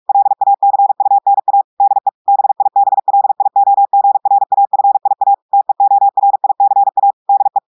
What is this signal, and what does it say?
Morse code, heard as a keyed tone, about 46 wpm, 800 Hz. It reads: LACUNA DE BIBLIOGRAFIA TEORICA DE
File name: portugais texte en clair 1 vitesse 45 mots minutes